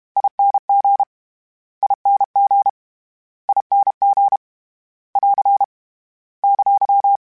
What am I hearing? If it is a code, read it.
Morse code, heard as a keyed tone, about 32 wpm, 800 Hz. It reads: ING ING ING <AR> !
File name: General / General-mp3s / General @30